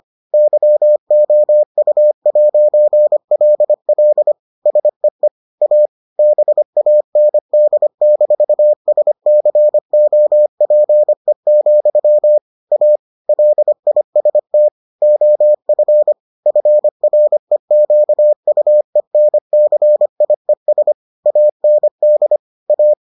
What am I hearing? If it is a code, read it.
Morse code, heard as a keyed tone, about 25 wpm, 600 Hz. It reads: YOU'LL SEE A BAND-SCOPE, A LIST OF FREQUENCIES AND A